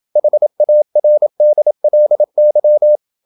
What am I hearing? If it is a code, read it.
Morse code, heard as a keyed tone, about 27 wpm, 600 Hz. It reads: HARDLY
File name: Never 0027wpm